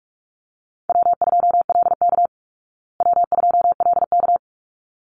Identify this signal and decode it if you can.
Morse code, heard as a keyed tone, about 44 wpm, 700 Hz. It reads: W2LK W2LK
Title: Call Signs Repeated 3x - 45wpm